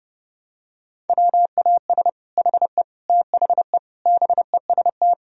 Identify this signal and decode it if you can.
Morse code, heard as a keyed tone, about 32 wpm, 700 Hz. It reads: WAS HE THE BEST
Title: Sentences from Top 1000 Words - Review Part 4 - 30wpm